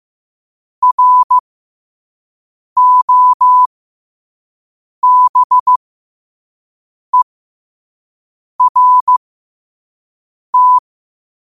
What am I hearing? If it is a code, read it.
Morse code, heard as a keyed tone, about 15 wpm, spaced as for 5 wpm, 1000 Hz. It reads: ROBERT
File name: qotdfiles / 5wpm